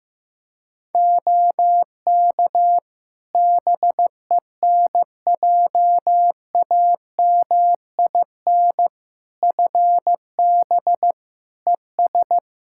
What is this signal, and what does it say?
Morse code, heard as a keyed tone, about 15 wpm, 700 Hz. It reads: OK BENJAMIN FB ES
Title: K1USN SST QSO Practice 15wpm